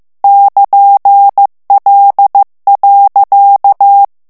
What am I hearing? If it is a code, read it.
Morse code, heard as a keyed tone, about 15 wpm, 800 Hz. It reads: <KN>L.
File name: morse mp3 / FRANCAIS TEXTE CLAIR 1 VITESSE 14 MOTS MINUTES